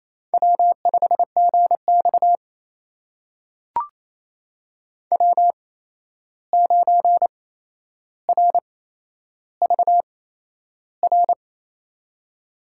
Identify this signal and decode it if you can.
Morse code, heard as a keyed tone, about 28 wpm, spaced as for 11 wpm, 700 Hz. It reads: W5GX W9RVR
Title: USA Call Signs - 8x Character Spacing - 28wpm